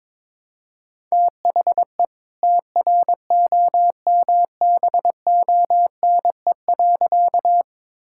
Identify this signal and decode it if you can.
Morse code, heard as a keyed tone, about 22 wpm, 690 Hz. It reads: THE TROMBONE.
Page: morse-code-ninja-practice-22wpm